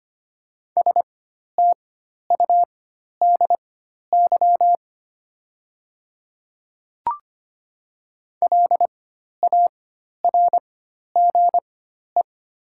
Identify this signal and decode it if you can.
Morse code, heard as a keyed tone, about 25 wpm, spaced as for 12 wpm, 690 Hz. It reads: STUDY LARGE
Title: 4x Character Spacing - Top 300 Words - 25wpm